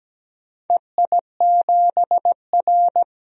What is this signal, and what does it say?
Morse code, heard as a keyed tone, about 17 wpm, 700 Hz. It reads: EI7R
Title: DXCC-100 Call Signs - 4 Characters - 17wpm